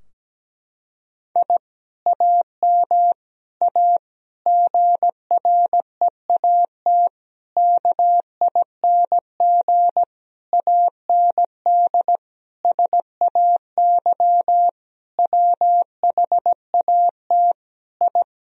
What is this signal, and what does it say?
Morse code, heard as a keyed tone, about 17 wpm, 700 Hz. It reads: I AM A GREAT KING AND SAY WHAT I
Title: Sentences from Top 1000 Words - Review Part 4 - 17wpm